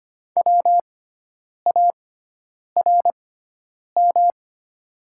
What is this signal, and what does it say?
Morse code, heard as a keyed tone, about 25 wpm, spaced as for 9 wpm, 700 Hz. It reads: WARM
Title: 6x Character Spacing - Top 500 Words - 25wpm